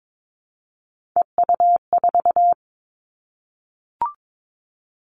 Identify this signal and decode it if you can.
Morse code, heard as a keyed tone, about 22 wpm, 700 Hz. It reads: EU4
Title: Callsign Prefixes for Top 100 DXCC - 22wpm